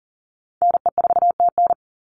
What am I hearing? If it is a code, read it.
Morse code, heard as a keyed tone, about 39 wpm, 700 Hz. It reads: NE4TN
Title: NA SOTA Chaser Callsigns 40wpm